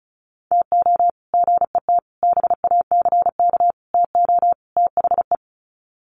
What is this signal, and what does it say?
Morse code, heard as a keyed tone, about 34 wpm, 700 Hz. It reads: TO GET BACK TO THE
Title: Sets of 5 Words Encore 35wpm